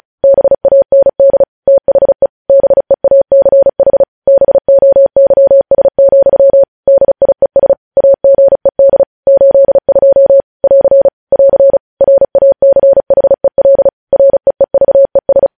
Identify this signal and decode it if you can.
Morse code, heard as a keyed tone, about 36 wpm, 550 Hz. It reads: BAND THE BEACH BOYS, DIES AGED 82 <AR> <AR> RACHEL REEVES
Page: news headlines in morse code at 35 wpm